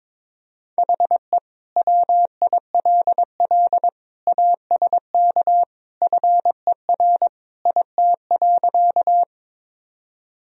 Morse code, heard as a keyed tone, about 22 wpm, 700 Hz. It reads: HE WILL ASK FER IT.
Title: Sentences from Top 200 Words and Abbreviations 22wpm